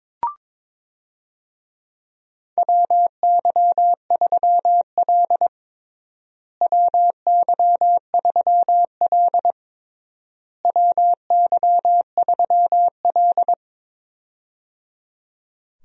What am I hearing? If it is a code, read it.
Morse code, heard as a keyed tone, about 22 wpm, 700 Hz. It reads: WY3L WY3L WY3L
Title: USA Call Signs - 4 Characters - Repeated 3x - 22wpm